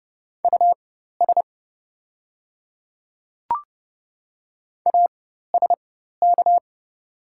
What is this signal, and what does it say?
Morse code, heard as a keyed tone, about 30 wpm, spaced as for 14 wpm, 700 Hz. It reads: US ASK
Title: 4x Character Spacing - Top 300 Words - 30wpm